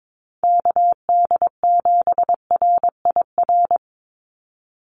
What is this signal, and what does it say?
Morse code, heard as a keyed tone, about 22 wpm, 700 Hz. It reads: KD7RIR
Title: USA Call Signs 22wpm